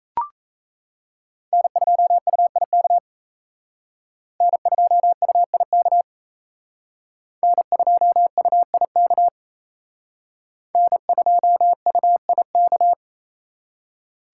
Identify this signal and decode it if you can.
Morse code, heard as a keyed tone, about 34 wpm, 700 Hz. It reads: N2UIK N2UIK N2UIK N2UIK
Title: Speed Racing - USA 1x3 Call Signs 28wpm